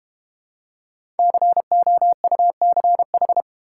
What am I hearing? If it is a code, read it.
Morse code, heard as a keyed tone, about 30 wpm, 700 Hz. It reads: COUCH
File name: Easy Words - Speed-Racing with Hints - 25wpm